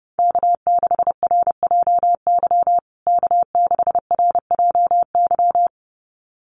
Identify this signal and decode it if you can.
Morse code, heard as a keyed tone, about 30 wpm, 700 Hz. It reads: K6RJY K6RJY
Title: USA Call Signs Repeated 2x 30wpm